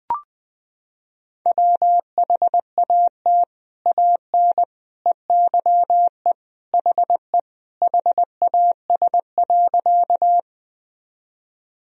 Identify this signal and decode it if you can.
Morse code, heard as a keyed tone, about 21 wpm, 700 Hz. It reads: WHAT AN EYE HE HAS.
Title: Sentences from Top 1000 Words - Review Part 1 - 20wpm